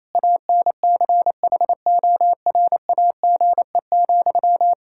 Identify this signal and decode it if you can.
Morse code, heard as a keyed tone, about 28 wpm, 700 Hz. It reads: ANCHORAGE,
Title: Practice QSO - Exchange 1 - Beginning - 28wpm